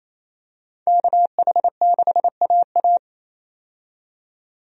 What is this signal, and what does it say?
Morse code, heard as a keyed tone, about 28 wpm, 700 Hz. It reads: KH6AA
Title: DXCC-100 Call Signs - 5 Characters - 28wpm